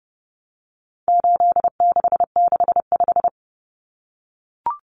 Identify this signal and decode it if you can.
Morse code, heard as a keyed tone, about 30 wpm, 700 Hz. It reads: 8665
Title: Numbers - 4-Digit 30wpm